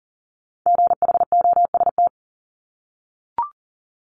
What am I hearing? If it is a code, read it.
Morse code, heard as a keyed tone, about 37 wpm, 700 Hz. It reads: GHOST
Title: Multi-letter Phonemes Repeated 3x 40wpm